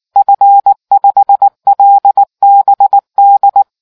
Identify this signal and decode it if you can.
Morse code, heard as a keyed tone, about 19 wpm, 800 Hz. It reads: F5LBD